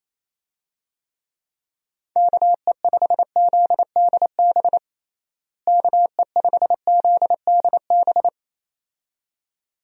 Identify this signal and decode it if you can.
Morse code, heard as a keyed tone, about 27 wpm, 700 Hz. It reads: KE5ZDB KE5ZDB
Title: USA Call Signs - 6 Characters - Repeated 2x - 28wpm